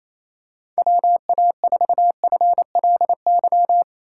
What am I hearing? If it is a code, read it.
Morse code, heard as a keyed tone, about 28 wpm, 700 Hz. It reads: WA4FLY
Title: USA Call Signs - 6 Characters - 4x Character Spacing - 28wpm